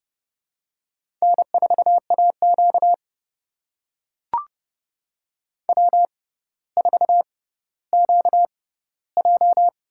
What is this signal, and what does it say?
Morse code, heard as a keyed tone, about 31 wpm, spaced as for 16 wpm, 700 Hz. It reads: N4AQ W4QJ
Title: USA Call Signs - 4 Characters - 6x Character Spacing - 30wpm